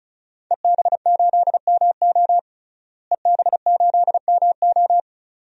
Morse code, heard as a keyed tone, about 36 wpm, 700 Hz. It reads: EB8MO EB8MO
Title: DXCC-100 Call Signs - 5 Characters - Repeated 2x - 35wpm